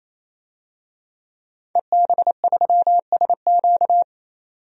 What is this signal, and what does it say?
Morse code, heard as a keyed tone, about 28 wpm, 700 Hz. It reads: EB3SQ
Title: DXCC-100 Call Signs - 5 Characters - 28wpm